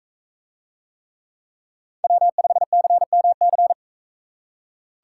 Morse code, heard as a keyed tone, about 42 wpm, 700 Hz. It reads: W5CMC